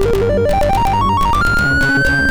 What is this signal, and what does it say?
There is no code.